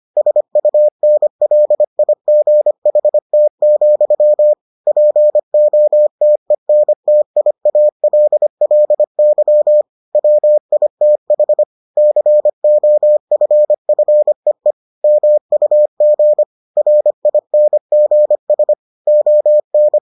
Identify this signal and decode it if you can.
Morse code, heard as a keyed tone, about 25 wpm, 600 Hz. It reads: SUNLIGHT, POTENTIALLY WITH COFFEE MUG RINGS ON